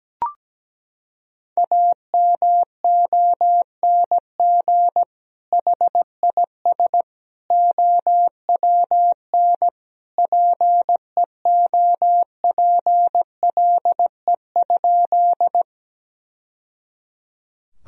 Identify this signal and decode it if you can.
Morse code, heard as a keyed tone, about 17 wpm, 700 Hz. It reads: AMONG HIS OWN PEOPLE?
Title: Sentences from Top 1000 Words - Review Part 3 - 17wpm